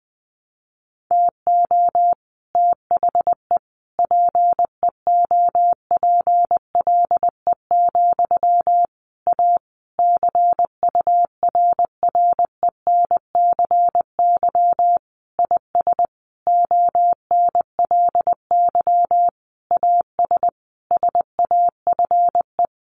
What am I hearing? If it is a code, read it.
Morse code, heard as a keyed tone, about 20 wpm, 700 Hz. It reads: TO THE PEOPLE, A CURRENCY IS ONLY AS SAFE
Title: Sentences from Top 1500 Words 20wpm